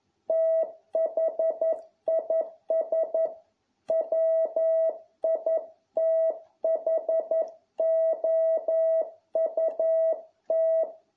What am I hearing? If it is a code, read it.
Morse code, heard as a keyed tone, about 11 wpm, 650 Hz. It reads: THIS WITHOUT